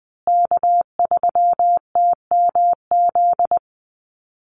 Morse code, heard as a keyed tone, about 20 wpm, 700 Hz. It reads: K3TMZ
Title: USA Call Signs Repeated 2x 20wpm